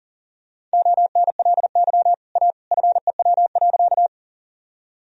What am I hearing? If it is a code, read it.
Morse code, heard as a keyed tone, about 40 wpm, 700 Hz. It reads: ONLY A FEW.
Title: Sentences from Top 1000 Words - Review Part 1 - 40wpm